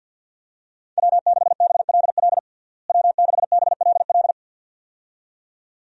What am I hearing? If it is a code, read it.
Morse code, heard as a keyed tone, about 53 wpm, 700 Hz. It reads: W6BLL W6BLL
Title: USA Call Signs - 5 Characters - Repeated 2x - 50wpm